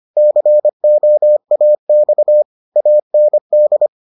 Morse code, heard as a keyed tone, about 25 wpm, 600 Hz. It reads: COAX AND